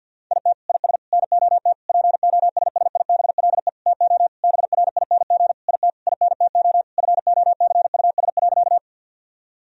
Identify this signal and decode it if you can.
Morse code, heard as a keyed tone, about 50 wpm, 700 Hz. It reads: IT IS NOT POSSIBLE TO BRING IT INTO FOCUS.
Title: Sentences from Top 1000 Words - Review Part 4 - 50wpm